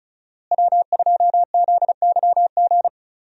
Morse code, heard as a keyed tone, about 36 wpm, 700 Hz. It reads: W2ZYG
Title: USA Call Signs - 5 Characters - Repeated 2x - 35wpm